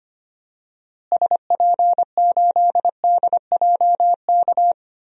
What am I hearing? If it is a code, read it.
Morse code, heard as a keyed tone, about 25 wpm, 700 Hz. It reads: SP8DJK
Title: DXCC-100 Call Signs - 6 Characters - Repeated 3x - 25wpm